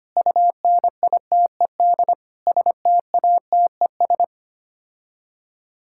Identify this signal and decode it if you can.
Morse code, heard as a keyed tone, about 25 wpm, 700 Hz. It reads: UNITED STATES
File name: Sets of 9 Words - 25wpm